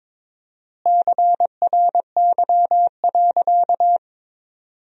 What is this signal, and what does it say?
Morse code, heard as a keyed tone, about 21 wpm, 700 Hz. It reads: CRY.
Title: 4x Word-Spacing - Sentences from Top 500 Words - 22wpm